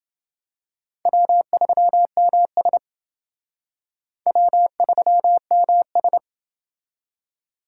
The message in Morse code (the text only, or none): W3MS W3MS